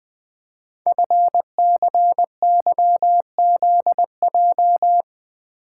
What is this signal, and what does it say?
Morse code, heard as a keyed tone, about 20 wpm, 700 Hz. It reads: FCYZJ
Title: Code Groups - 5 Letters - 20wpm